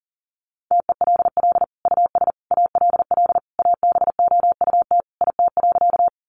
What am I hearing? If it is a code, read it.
Morse code, heard as a keyed tone, about 40 wpm, 700 Hz. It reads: TELL US ALL ABOUT IT.
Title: Sentences from Top 200 Words 40wpm